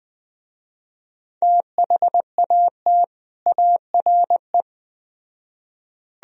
Morse code, heard as a keyed tone, about 20 wpm, 700 Hz. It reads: THAT ARE
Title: Speed-Racing - Sets of 2 Words 17wpm